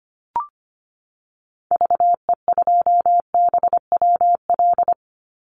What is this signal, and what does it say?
Morse code, heard as a keyed tone, about 25 wpm, 700 Hz. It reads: VE2BWL
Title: Call Signs Repeated 3x - 25wpm